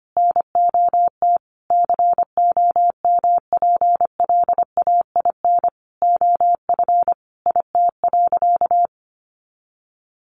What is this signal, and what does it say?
Morse code, heard as a keyed tone, about 25 wpm, 700 Hz. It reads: NOT COMPLAIN OF IT.
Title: Sentences from Top 1300 Words 25wpm